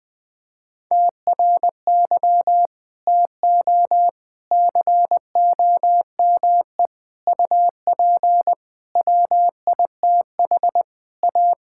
Morse code, heard as a keyed tone, about 21 wpm, 700 Hz. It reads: TRY TO COME UP WITH A
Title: Sentences with 4-Word Phrases - 20wpm